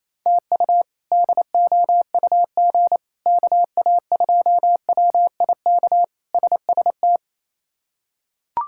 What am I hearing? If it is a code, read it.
Morse code, heard as a keyed tone, about 28 wpm, 700 Hz. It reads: TU DOUG KA2WIK SST
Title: K1USN SST QSO Practice - 28wpm